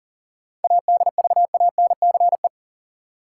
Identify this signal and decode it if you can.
Morse code, heard as a keyed tone, about 40 wpm, 700 Hz. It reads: ADVANCE
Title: Sentences from Top 1400 Words - New Word Drill Alt - 40wpm